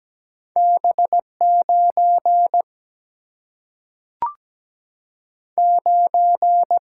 Morse code, heard as a keyed tone, about 17 wpm, 700 Hz. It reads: B 9 9
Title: Single-Letter Single-Number 17wpm